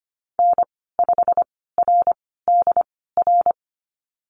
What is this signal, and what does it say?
Morse code, heard as a keyed tone, about 25 wpm, spaced as for 16 wpm, 700 Hz. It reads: N5RDR
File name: Lesson 15g - Number 5 - Call Signs 25wpm